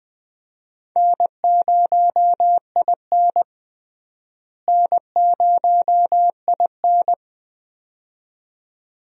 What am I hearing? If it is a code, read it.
Morse code, heard as a keyed tone, about 20 wpm, 700 Hz. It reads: N0IN N0IN
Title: USA Call Signs - 4 Characters - Repeated 2x - 20wpm